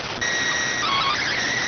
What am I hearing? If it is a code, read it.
SSTV (slow-scan television), PD 290.